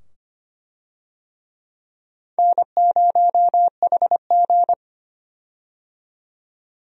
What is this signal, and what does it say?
Morse code, heard as a keyed tone, about 25 wpm, 700 Hz. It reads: N0HG
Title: USA Call Signs - 4 Characters - 25wpm